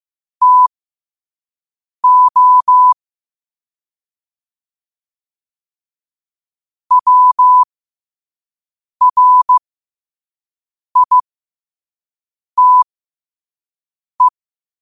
Morse code, heard as a keyed tone, about 15 wpm, spaced as for 5 wpm, 1000 Hz. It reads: TO WRITE
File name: qotdfiles / 5wpm